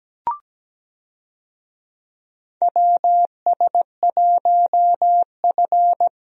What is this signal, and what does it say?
Morse code, heard as a keyed tone, about 17 wpm, 700 Hz. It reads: WS1F
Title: USA Call Signs - 4 Characters - Repeated 2x - 17wpm